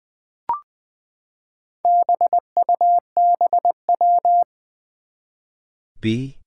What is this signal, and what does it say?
Morse code, heard as a keyed tone, about 21 wpm, 700 Hz. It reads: BUBW
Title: Four Challenging Letter-Number Combinations 20wpm